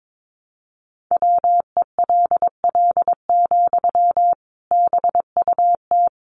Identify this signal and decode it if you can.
Morse code, heard as a keyed tone, about 22 wpm, 700 Hz. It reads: WELL, BUT
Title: Sentences from Top 500 Words 22wpm